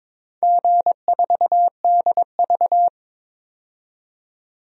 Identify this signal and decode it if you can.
Morse code, heard as a keyed tone, about 22 wpm, 700 Hz. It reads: G4DV
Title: DXCC-100 Call Signs - 4 Characters - 22wpm